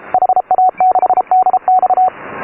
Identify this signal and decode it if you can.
Morse code, heard as a keyed tone, about 29 wpm, 720 Hz. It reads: HA6DX
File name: AUDIO2018 / HA3FMR CW